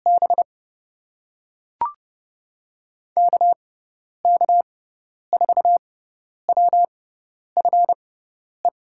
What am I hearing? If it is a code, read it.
Morse code, heard as a keyed tone, about 30 wpm, spaced as for 10 wpm, 700 Hz. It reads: B KK4WFE
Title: USA Call Signs - 6 Characters - 6x Character Spacing - 30wpm